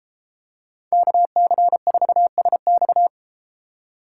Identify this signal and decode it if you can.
Morse code, heard as a keyed tone, about 31 wpm, 700 Hz. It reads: KC4SX